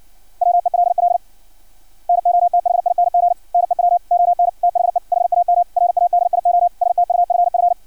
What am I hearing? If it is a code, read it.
Morse code, heard as a keyed tone, about 56 wpm, 700 Hz. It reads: WELL NOTHING NEW ON THE HAM RADIO STUFF
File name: archived / CRSnet